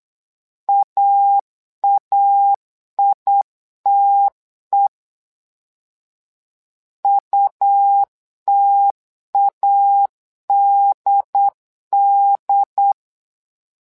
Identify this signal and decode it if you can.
Morse code, heard as a keyed tone, about 8 wpm, 800 Hz. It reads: AAITE UTADD
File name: LEZIONI / 2 UHDN / 2 UHD 8wpm 2258